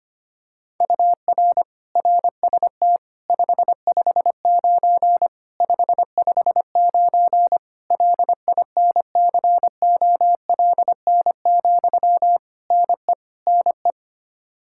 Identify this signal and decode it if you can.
Morse code, heard as a keyed tone, about 25 wpm, 700 Hz. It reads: UR RST 559 559 LINCOLN, NE NE
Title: Practice QSO - Exchange 1 - Beginning - 25wpm